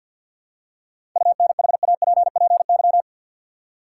Morse code, heard as a keyed tone, about 49 wpm, 700 Hz. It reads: UNHAPPY
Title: Seven Letter Words - 50wpm